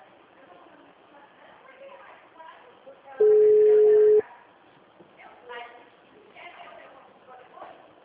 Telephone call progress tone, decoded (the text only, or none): ringback tone